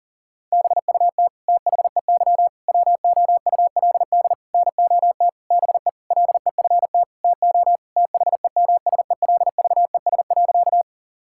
Morse code, heard as a keyed tone, about 37 wpm, 700 Hz. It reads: BUT THEY WOULD NOT BE LEFT TO THEMSELVES.
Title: Sentences from Top 1000 Words - Review Part 5 - 40wpm